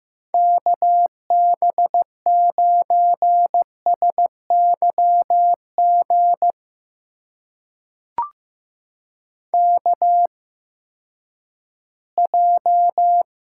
Morse code, heard as a keyed tone, about 15 wpm, spaced as for 10 wpm, 700 Hz. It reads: KB9SYG KJ
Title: USA Call Signs - 6 Characters - 8x Character Spacing - 15wpm